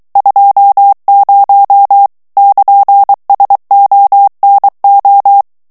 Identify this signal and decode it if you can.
Morse code, heard as a keyed tone, about 23 wpm, 800 Hz. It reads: 20 <KN>SONO
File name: morse mp3 / ITALIEN texte clair 1 vitesse 22 mots minute